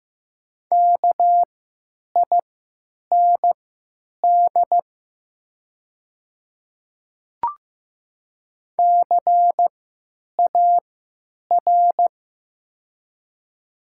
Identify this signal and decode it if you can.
Morse code, heard as a keyed tone, about 15 wpm, spaced as for 9 wpm, 700 Hz. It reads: KIND CAR
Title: 3x Character Spacing - Top 300 Words - 15wpm